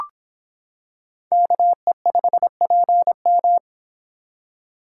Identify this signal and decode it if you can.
Morse code, heard as a keyed tone, about 26 wpm, 700 Hz. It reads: KE5PM